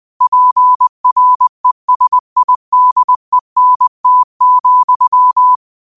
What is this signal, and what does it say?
Morse code, heard as a keyed tone, about 20 wpm, 1000 Hz. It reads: PRESIDENT,
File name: qotdfiles / 20wpm